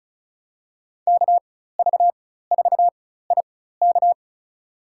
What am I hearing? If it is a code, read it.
Morse code, heard as a keyed tone, about 36 wpm, spaced as for 16 wpm, 700 Hz. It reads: KV4IK
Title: USA Call Signs - 5 Characters - 4x Character Spacing - 35wpm